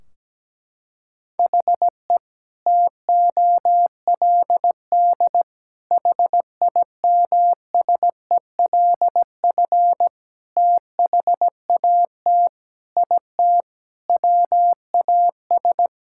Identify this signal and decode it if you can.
Morse code, heard as a keyed tone, about 17 wpm, 700 Hz. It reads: HE TOLD HIMSELF THAT IT WAS
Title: Sentences from Top 1600 Words 17wpm